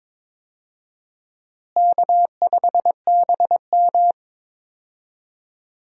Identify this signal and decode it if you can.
Morse code, heard as a keyed tone, about 22 wpm, 700 Hz. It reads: K5BM
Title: USA Call Signs - 4 Characters - Repeated 3x - 22wpm